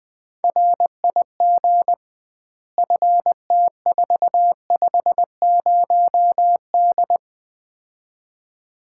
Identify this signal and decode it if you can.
Morse code, heard as a keyed tone, about 20 wpm, 700 Hz. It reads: RIG FT450D
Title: Common QSO Phrases - 2x Word-Spacing - 20wpm